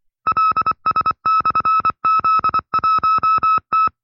Morse code, heard as a keyed tone, about 24 wpm, 1310 Hz. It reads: LS/Z1T
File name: FreeFiles / CW-practice / 20WPM-side2